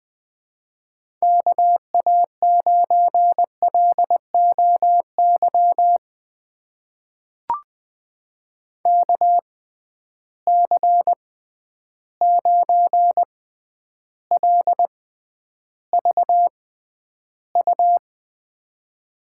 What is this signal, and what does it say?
Morse code, heard as a keyed tone, about 20 wpm, spaced as for 10 wpm, 700 Hz. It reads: KA9LOY KC9LVU
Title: USA Call Signs - 6 Characters - 6x Character Spacing - 20wpm